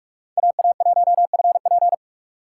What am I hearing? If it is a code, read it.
Morse code, heard as a keyed tone, about 46 wpm, 700 Hz. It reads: AA1FP
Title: USA Call Signs - 2x Character Spacing - 45wpm